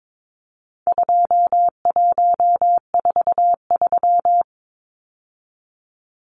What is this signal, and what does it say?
Morse code, heard as a keyed tone, about 22 wpm, 700 Hz. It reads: 2143
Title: Numbers - 4-Digit 22wpm